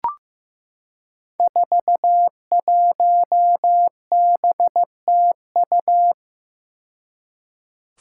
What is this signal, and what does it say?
Morse code, heard as a keyed tone, about 15 wpm, 700 Hz. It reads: 41BTU
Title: Five Challenging Letter-Number Combinations 15wpm